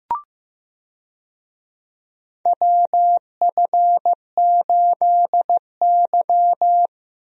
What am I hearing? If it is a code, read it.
Morse code, heard as a keyed tone, about 16 wpm, 700 Hz. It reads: WF8Y
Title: USA Call Signs - 4 Characters - Repeated 2x - 15wpm